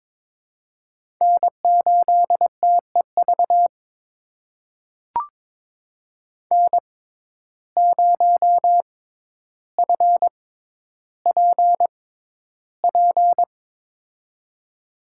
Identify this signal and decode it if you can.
Morse code, heard as a keyed tone, about 22 wpm, spaced as for 11 wpm, 700 Hz. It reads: N8TEV N0FPP
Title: USA Call Signs - 6x Character Spacing - 22wpm